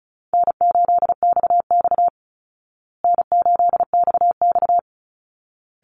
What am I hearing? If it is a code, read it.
Morse code, heard as a keyed tone, about 34 wpm, 700 Hz. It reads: N8XX N8XX
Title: Easy Call Signs Repeated 2x - 35wpm